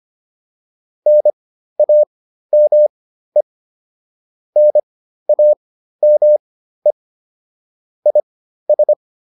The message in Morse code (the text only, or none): NAME NAME IS